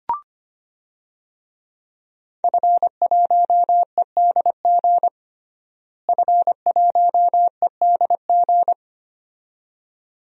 Morse code, heard as a keyed tone, about 25 wpm, 700 Hz. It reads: F1EDG F1EDG
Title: DXCC-100 Call Signs - 5 Characters - Repeated 2x - 25wpm